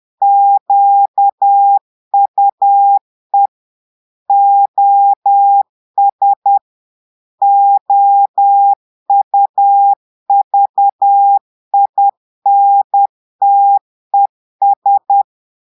Morse code, heard as a keyed tone, about 10 wpm, 800 Hz. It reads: QUE OS OUVINTES
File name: portugais texte en clair 2 vitesse 10 mots minutes